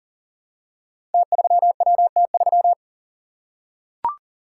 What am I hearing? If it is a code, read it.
Morse code, heard as a keyed tone, about 40 wpm, 700 Hz. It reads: T3WT3
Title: Code Groups - 5 Letter:Numbers - 40wpm